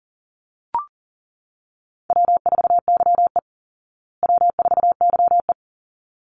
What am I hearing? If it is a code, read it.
Morse code, heard as a keyed tone, about 40 wpm, 700 Hz. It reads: W4YE W4YE
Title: Easy Call Signs Repeated 2x - 40wpm